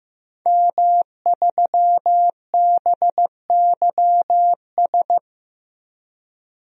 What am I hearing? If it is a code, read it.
Morse code, heard as a keyed tone, about 15 wpm, 700 Hz. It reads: M3BYS
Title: DXCC-100 Call Signs - 5 Characters - Repeated 2x - 15wpm